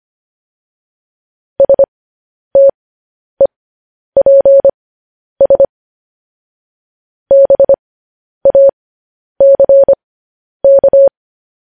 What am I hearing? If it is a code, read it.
Morse code, heard as a keyed tone, about 25 wpm, spaced as for 10 wpm, 550 Hz. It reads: STEPS BACK
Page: news-headlines-in-morse-code-at-10-wpm